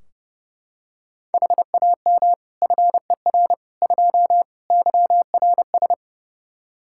Morse code, heard as a keyed tone, about 32 wpm, 700 Hz. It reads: HAM FER 2 YRS
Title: Practice QSO - Exchange 3 - More Info - 30wpm